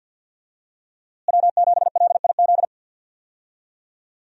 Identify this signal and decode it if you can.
Morse code, heard as a keyed tone, about 46 wpm, 700 Hz. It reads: W7LIZ
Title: USA Call Signs - 5 Characters - Repeated 2x - 50wpm